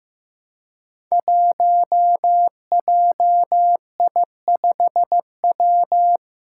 Morse code, heard as a keyed tone, about 15 wpm, 700 Hz. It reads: 1JI5W
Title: Five Challenging Letter-Number Combinations 15wpm